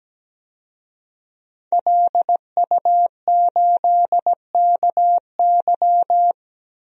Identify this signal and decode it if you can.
Morse code, heard as a keyed tone, about 17 wpm, 700 Hz. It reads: LU8KY
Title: DXCC-100 Call Signs - 5 Characters - 17wpm